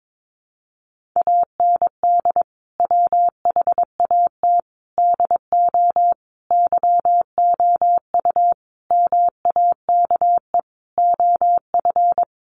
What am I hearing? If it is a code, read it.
Morse code, heard as a keyed tone, about 22 wpm, 700 Hz. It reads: AND WHAT DO YOU MAKE OF
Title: Sentences from Top 100 Words 22wpm